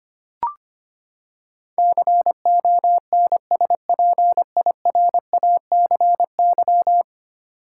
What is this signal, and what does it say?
Morse code, heard as a keyed tone, about 25 wpm, 700 Hz. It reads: CONSPIRACY
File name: Ten Letter Words - 25wpm